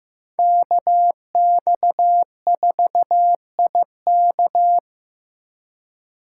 Morse code, heard as a keyed tone, about 15 wpm, 700 Hz. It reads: KX4IK
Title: USA Call Signs - 5 Characters - 4x Character Spacing - 15wpm